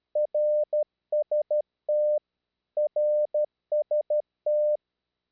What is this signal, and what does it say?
Morse code, heard as a keyed tone, about 12 wpm, 600 Hz. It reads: RST RST
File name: eme / rw3bp